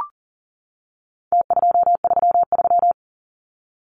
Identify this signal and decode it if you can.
Morse code, heard as a keyed tone, about 40 wpm, 700 Hz. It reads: T233
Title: Numbers - Contest Style - 4-Digit 40wpm